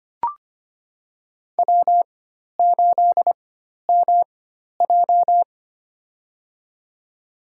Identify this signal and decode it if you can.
Morse code, heard as a keyed tone, about 25 wpm, spaced as for 12 wpm, 700 Hz. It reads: W8MJ
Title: USA Call Signs - 4 Characters - 4x Character Spacing - 25wpm